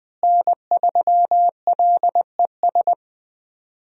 Morse code, heard as a keyed tone, about 21 wpm, 700 Hz. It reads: N3LES
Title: USA Call Signs - 5 Characters - Repeated 2x - 20wpm